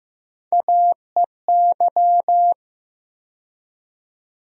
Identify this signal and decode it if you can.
Morse code, heard as a keyed tone, about 15 wpm, 700 Hz. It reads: AEY
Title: USA Call Sign Postfixes - 3 Letters 15wpm